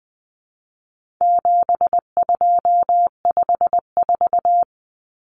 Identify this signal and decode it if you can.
Morse code, heard as a keyed tone, about 20 wpm, 700 Hz. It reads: 7254
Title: Common Cut Numbers - 4-Digit 20wpm